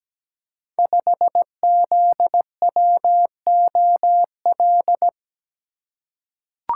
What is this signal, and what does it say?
Morse code, heard as a keyed tone, about 17 wpm, 700 Hz. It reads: 5ZWOL
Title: Code Groups - 5 Letter:Numbers - 17wpm